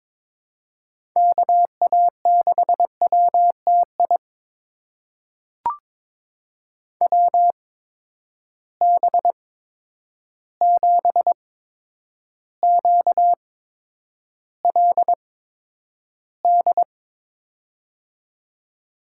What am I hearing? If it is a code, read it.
Morse code, heard as a keyed tone, about 22 wpm, spaced as for 9 wpm, 700 Hz. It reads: KA6WTI WB7QLD
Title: USA Call Signs - 6 Characters - 8x Character Spacing - 22wpm